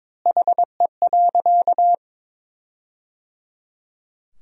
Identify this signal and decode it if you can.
Morse code, heard as a keyed tone, about 22 wpm, 700 Hz. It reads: HE.
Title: 2x Word-Spacing - Sentences from Top 400 Words - 22wpm